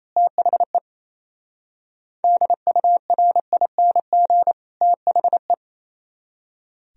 Morse code, heard as a keyed tone, about 28 wpm, 700 Hz. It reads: THE DURING THE
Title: Speed-Racing - Sets of 2 Words 28wpm